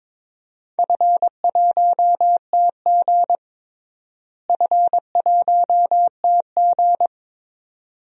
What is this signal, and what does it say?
Morse code, heard as a keyed tone, about 22 wpm, 700 Hz. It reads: F1TG F1TG
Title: DXCC-100 Call Signs - 4 Characters - Repeated 2x - 22wpm